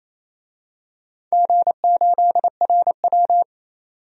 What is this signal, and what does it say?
Morse code, heard as a keyed tone, about 29 wpm, 700 Hz. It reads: G8RW
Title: DXCC-100 Call Signs - 4 Characters - 28wpm